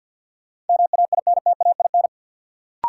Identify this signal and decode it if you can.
Morse code, heard as a keyed tone, about 51 wpm, 700 Hz. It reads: MAINTAIN
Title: Sentences from Top 1000 Words - Review Part 5 - New Word Drill - 50wpm